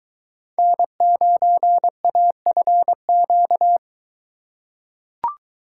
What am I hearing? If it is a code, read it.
Morse code, heard as a keyed tone, about 23 wpm, 700 Hz. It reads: N9AFQ